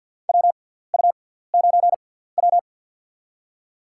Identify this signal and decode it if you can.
Morse code, heard as a keyed tone, about 48 wpm, spaced as for 17 wpm, 700 Hz. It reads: WU9W
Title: USA Call Signs - 6x Character Spacing - 50wpm